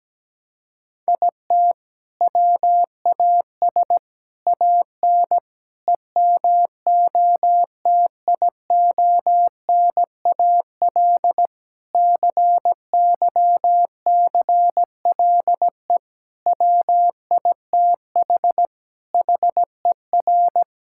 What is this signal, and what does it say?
Morse code, heard as a keyed tone, about 17 wpm, 700 Hz. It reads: IT WAS AN EMOTIONAL CYCLE WITH HER
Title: Sentences from Top 1000 Words - Review Part 6 - 17wpm